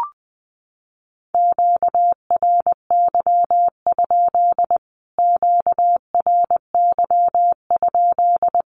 Morse code, heard as a keyed tone, about 20 wpm, 700 Hz. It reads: QRY? QRY?
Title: Q-Codes Repeated 3x 20wpm